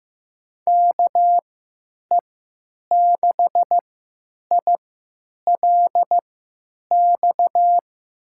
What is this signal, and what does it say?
Morse code, heard as a keyed tone, about 15 wpm, spaced as for 9 wpm, 700 Hz. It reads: KE6ILX
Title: USA Call Signs - 3x Character Spacing - 15wpm